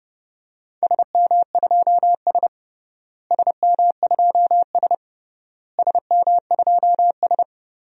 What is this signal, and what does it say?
Morse code, heard as a keyed tone, about 31 wpm, 700 Hz. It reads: SM2S SM2S SM2S
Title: DXCC-100 Call Signs - 4 Characters - Repeated 3x - 30wpm